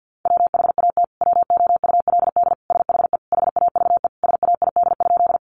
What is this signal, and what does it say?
Morse code, heard as a keyed tone, about 50 wpm, 700 Hz. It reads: WHAT WOULD SHE HAVE SAID?
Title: Sentences from Top 100 Words 50wpm